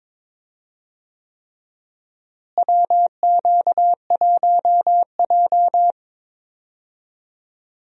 Morse code, heard as a keyed tone, about 22 wpm, 700 Hz. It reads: WQ1J
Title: USA Call Signs - 4 Characters - Repeated 3x - 22wpm